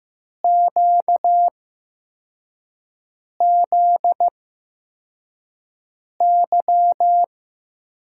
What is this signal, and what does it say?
Morse code, heard as a keyed tone, about 15 wpm, spaced as for 4 wpm, 700 Hz. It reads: QZY
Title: USA Call Signs - 5 Characters - 8x Character Spacing - 15wpm